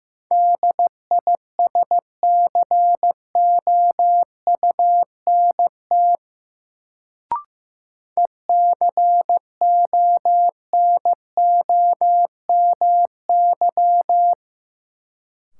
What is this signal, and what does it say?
Morse code, heard as a keyed tone, about 15 wpm, 700 Hz. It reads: DISCOUNT ECONOMY
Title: Sentences from Top 1200 Words - New Word Drill Alt - 15wpm